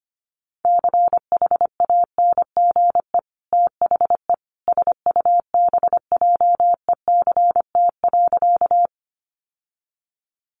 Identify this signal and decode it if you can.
Morse code, heard as a keyed tone, about 25 wpm, 700 Hz. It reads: CHANGE THE SUBJECT.
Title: Sentences from Top 800 Words 25wpm